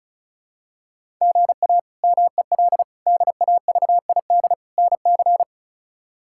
Morse code, heard as a keyed tone, about 36 wpm, 700 Hz. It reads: GA MEL DAVID NC
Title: K1USN SST QSO Practice - 35wpm